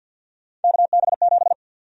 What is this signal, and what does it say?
Morse code, heard as a keyed tone, about 47 wpm, 700 Hz. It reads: KB7
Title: USA Call Sign Prefixes - 50wpm